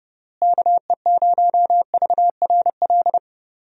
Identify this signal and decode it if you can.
Morse code, heard as a keyed tone, about 30 wpm, 700 Hz. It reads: KE0VRL
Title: USA Call Signs - 6 Characters - Repeated 3x - 30wpm